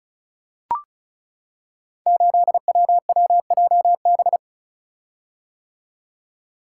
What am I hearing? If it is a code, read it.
Morse code, heard as a keyed tone, about 34 wpm, 700 Hz. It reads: 8WWJB
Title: Code Groups - 5 Letter:Numbers - 35wpm